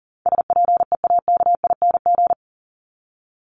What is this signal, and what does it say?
Morse code, heard as a keyed tone, about 40 wpm, 700 Hz. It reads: SPEAKING
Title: ING Suffix - Top 100 Words with Base Words 40wpm